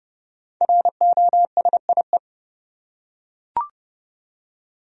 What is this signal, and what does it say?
Morse code, heard as a keyed tone, about 32 wpm, 700 Hz. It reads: ROSIE
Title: USA Common Names - Max 5 Letters - 30wpm